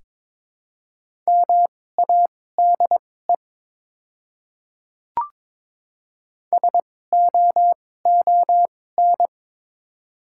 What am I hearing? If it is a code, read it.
Morse code, heard as a keyed tone, about 22 wpm, spaced as for 16 wpm, 700 Hz. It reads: MADE SOON
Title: 2x Character Spacing - Top 500 Words - 22wpm